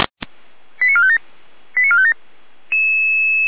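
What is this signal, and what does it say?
Five-tone selective call (ZVEI), heard as quick runs of five tones, tones of about 70 ms. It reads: 89457 89457
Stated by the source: ZVEI five-tone call for address 85386, received by SDR.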